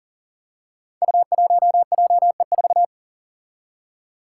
Morse code, heard as a keyed tone, about 39 wpm, 700 Hz. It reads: U1JE4
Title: Five Challenging Letter-Number Combinations 40wpm